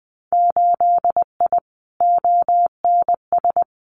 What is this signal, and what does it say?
Morse code, heard as a keyed tone, about 20 wpm, 700 Hz. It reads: 8I ONS
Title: Winter Field Day Exchanges Encore 20wpm